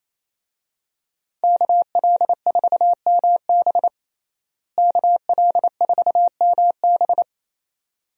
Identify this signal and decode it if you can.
Morse code, heard as a keyed tone, about 28 wpm, 700 Hz. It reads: KL4MB KL4MB
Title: USA Call Signs - 5 Characters - Repeated 2x - 28wpm